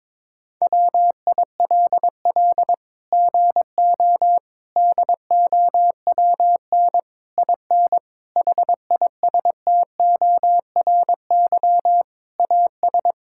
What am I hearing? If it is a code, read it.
Morse code, heard as a keyed tone, about 22 wpm, 700 Hz. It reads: WILL GO DOWN IN HISTORY AS
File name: Sets of 9 Words - 22wpm